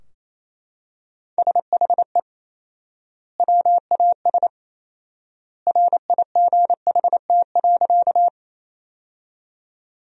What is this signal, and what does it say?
Morse code, heard as a keyed tone, about 28 wpm, 700 Hz. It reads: SHE WAS RIGHT.
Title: 4x Word-Spacing - Sentences from Top 200 Words - 28wpm